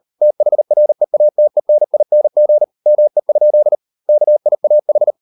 Morse code, heard as a keyed tone, about 38 wpm, 600 Hz. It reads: THREATENING ME? KIAH